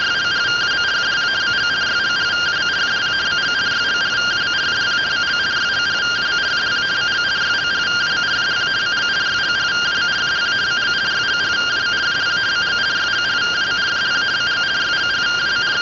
Two-tone teleprinter signal, about 170 Hz shift.